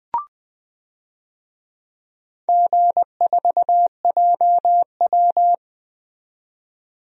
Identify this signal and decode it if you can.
Morse code, heard as a keyed tone, about 20 wpm, 700 Hz. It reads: G4JW
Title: DXCC-100 Call Signs - 4 Characters - 20wpm